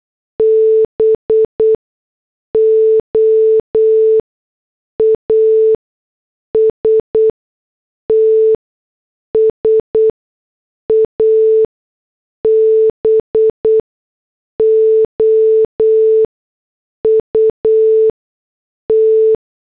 Morse code, heard as a keyed tone, about 8 wpm, spaced as for 6 wpm, 440 Hz. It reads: BOASTS ABOUT
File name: morse / headlines